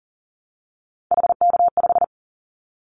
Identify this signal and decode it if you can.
Morse code, heard as a keyed tone, about 41 wpm, 700 Hz. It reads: HK5
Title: Callsign Prefixes for Top 100 DXCC - 40wpm